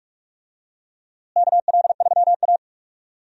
Morse code, heard as a keyed tone, about 44 wpm, 700 Hz. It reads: KL3A
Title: USA Call Signs - 4 Characters - Repeated 2x - 45wpm